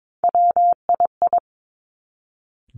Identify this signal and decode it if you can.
Morse code, heard as a keyed tone, about 22 wpm, 700 Hz. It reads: WII
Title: Three Challenging Letter-Number Combinations 22wpm